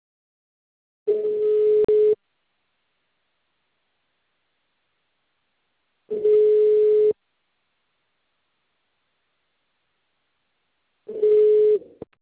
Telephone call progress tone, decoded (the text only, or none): ringback tone